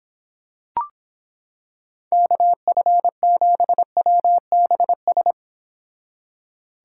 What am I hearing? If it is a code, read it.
Morse code, heard as a keyed tone, about 26 wpm, 700 Hz. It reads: KF7WBS